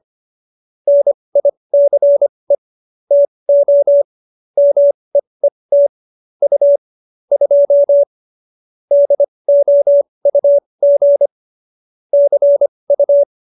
Morse code, heard as a keyed tone, about 25 wpm, spaced as for 20 wpm, 570 Hz. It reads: NICE TO MEET U 2 DOUG CU